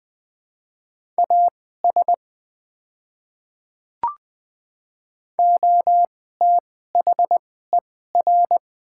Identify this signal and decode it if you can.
Morse code, heard as a keyed tone, about 20 wpm, spaced as for 14 wpm, 700 Hz. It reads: AS OTHER
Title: 2x Character Spacing - Top 100 Words - 20wpm